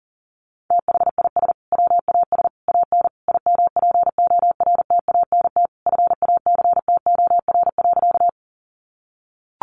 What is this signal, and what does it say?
Morse code, heard as a keyed tone, about 37 wpm, 700 Hz. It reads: THIS WAS AN IMPORTANT FACTOR.
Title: Sentences from Top 1300 Words 40wpm